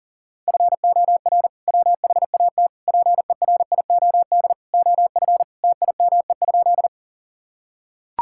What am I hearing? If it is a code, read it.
Morse code, heard as a keyed tone, about 40 wpm, 700 Hz. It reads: FOR WHAT PERIOD OF TIME?
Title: Sentences from Top 1000 Words - Review Part 5 - 40wpm